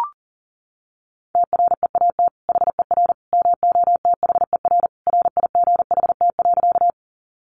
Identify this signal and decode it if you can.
Morse code, heard as a keyed tone, about 40 wpm, 700 Hz. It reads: TREAT HER MOTHER RIGHT.
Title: Sentences from Top 1300 Words 40wpm